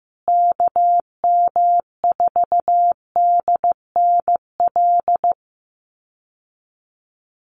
Morse code, heard as a keyed tone, about 15 wpm, 700 Hz. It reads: KM4DNL
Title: USA Call Signs Repeated 3x 15wpm